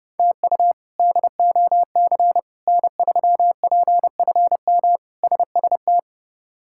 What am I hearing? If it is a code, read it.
Morse code, heard as a keyed tone, about 30 wpm, 700 Hz. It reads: TU DOC N3PFM SST
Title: K1USN SST QSO Practice - 30wpm